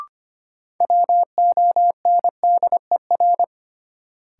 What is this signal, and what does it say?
Morse code, heard as a keyed tone, about 25 wpm, 700 Hz. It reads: WONDER
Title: Sentences from Top 500 Words - New Word Drill - 25wpm